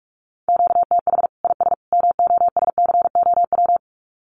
Morse code, heard as a keyed tone, about 44 wpm, 700 Hz. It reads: QTH IS MOSCOW
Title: Common QSO Phrases 45wpm